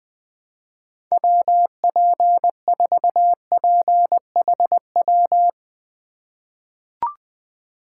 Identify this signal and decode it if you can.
Morse code, heard as a keyed tone, about 20 wpm, 700 Hz. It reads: WP4PHW
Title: USA Call Signs - 6 Characters - 2x Character Spacing - 20wpm